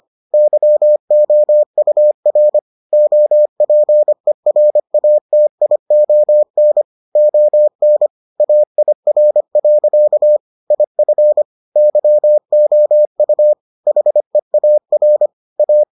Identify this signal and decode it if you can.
Morse code, heard as a keyed tone, about 25 wpm, 600 Hz. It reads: YOUR OPERATION ON AIR. IF YOU HEAR A